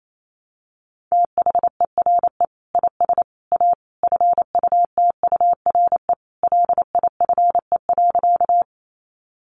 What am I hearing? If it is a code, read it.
Morse code, heard as a keyed tone, about 28 wpm, 700 Hz. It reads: THERE IS A FUTURE LIFE.
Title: Sentences from Top 700 Words 28wpm